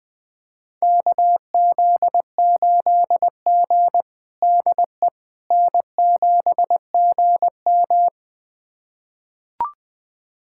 Morse code, heard as a keyed tone, about 20 wpm, 700 Hz. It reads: KZ8G DE N7GM
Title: USA Callsign DE Callsign - 4 Characters - 20wpm